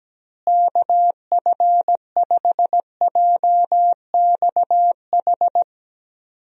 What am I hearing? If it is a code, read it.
Morse code, heard as a keyed tone, about 17 wpm, 700 Hz. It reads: KF5JXH
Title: USA Call Signs - 6 Characters - Repeated 2x - 17wpm